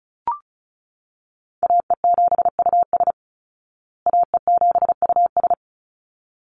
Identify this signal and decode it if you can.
Morse code, heard as a keyed tone, about 34 wpm, 700 Hz. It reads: AE7US AE7US
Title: Call Signs Repeated 3x - 35wpm